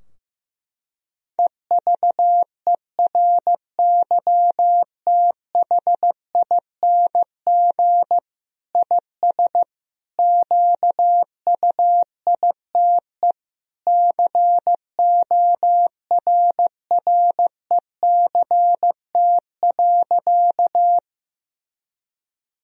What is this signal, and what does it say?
Morse code, heard as a keyed tone, about 15 wpm, 700 Hz. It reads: EVERYTHING IS QUITE CORRECT.
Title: Sentences from Top 1000 Words - Review Part 2 - 15wpm